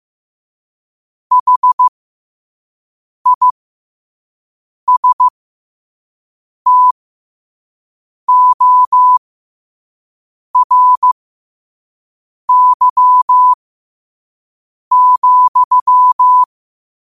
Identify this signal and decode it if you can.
Morse code, heard as a keyed tone, about 15 wpm, spaced as for 5 wpm, 1000 Hz. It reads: HISTORY,
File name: qotdfiles / 5wpm